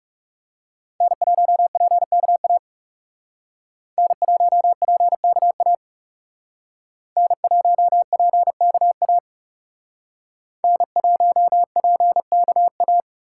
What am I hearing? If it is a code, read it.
Morse code, heard as a keyed tone, about 39 wpm, 700 Hz. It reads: N1PKA N1PKA N1PKA N1PKA